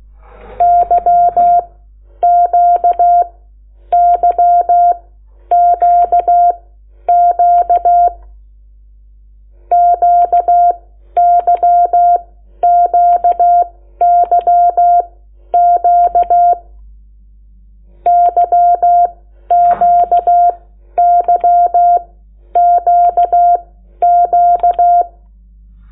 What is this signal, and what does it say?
Morse code, heard as a keyed tone, about 16 wpm, spaced as for 11 wpm, 700 Hz. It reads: YQYQQ QYQYQ YQYQQ
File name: Lecciones / CLASE Nº8